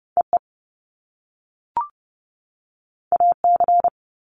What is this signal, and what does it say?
Morse code, heard as a keyed tone, about 30 wpm, 700 Hz. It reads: EE AC
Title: Top Two Letter Combinations in Words 30wpm